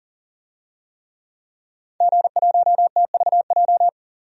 Morse code, heard as a keyed tone, about 40 wpm, 700 Hz. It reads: G1TVJ
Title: DXCC-100 Call Signs - 5 Characters - 40wpm